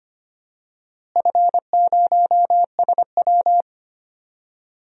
Morse code, heard as a keyed tone, about 25 wpm, 700 Hz. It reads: F0SW
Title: DXCC-100 Call Signs - 4 Characters - Repeated 3x - 25wpm